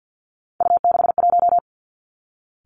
Morse code, heard as a keyed tone, about 50 wpm, 700 Hz. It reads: U61
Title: Three Challenging Letter-Number Combinations 50wpm